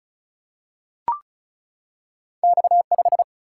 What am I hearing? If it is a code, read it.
Morse code, heard as a keyed tone, about 35 wpm, 700 Hz. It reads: X5
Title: Single-Letter Single-Number 35wpm